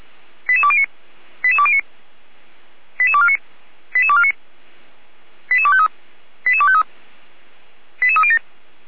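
Five-tone selective call (ZVEI), heard as quick runs of five tones, tones of about 70 ms. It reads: 88209 88209 88249 88249 88253 88253 88307
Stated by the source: ZVEI five-tone call received via SDR.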